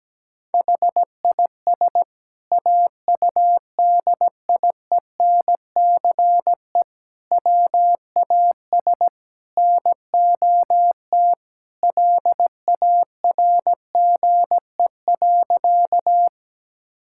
Morse code, heard as a keyed tone, about 17 wpm, 700 Hz. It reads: HIS AUDIENCE WAS NOT LARGE.
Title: Sentences from Top 1000 Words - Review Part 5 - 17wpm